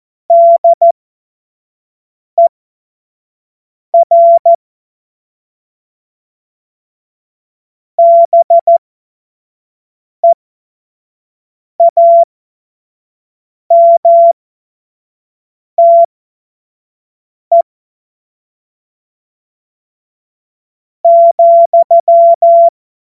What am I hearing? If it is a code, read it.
Morse code, heard as a keyed tone, about 14 wpm, spaced as for 4 wpm, 680 Hz. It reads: DER BEAMTE,
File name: Morsekurs / Witze / Witz 25. April 2023 25 F70